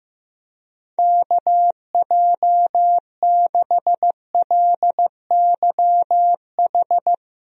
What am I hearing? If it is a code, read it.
Morse code, heard as a keyed tone, about 15 wpm, 700 Hz. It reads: KJ6LYH
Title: USA Call Signs - 6 Characters - Repeated 2x - 15wpm